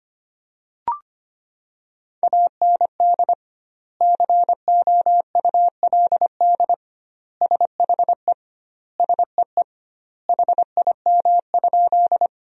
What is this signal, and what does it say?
Morse code, heard as a keyed tone, about 25 wpm, 700 Hz. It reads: AND COULD SHE SEE HIM?
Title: 2x Word-Spacing - Sentences from Top 100 Words - 25wpm